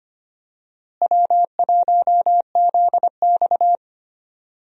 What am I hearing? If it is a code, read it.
Morse code, heard as a keyed tone, about 25 wpm, 700 Hz. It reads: W1ZX
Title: USA Call Signs - 4 Characters - Repeated 2x - 25wpm